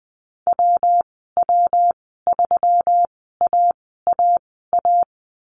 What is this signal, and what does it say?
Morse code, heard as a keyed tone, about 20 wpm, spaced as for 15 wpm, 700 Hz. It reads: WW3AAA
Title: 122 Lesson 20g - Number 3 - Call Signs 20wpm